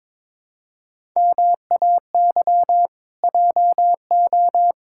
Morse code, heard as a keyed tone, about 21 wpm, 700 Hz. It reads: MAY JO
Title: Pangram Challenge v2 - 22wpm